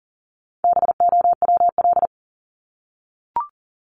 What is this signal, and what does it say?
Morse code, heard as a keyed tone, about 40 wpm, 700 Hz. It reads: BOWL
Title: Top 1001-1250 Words 40wpm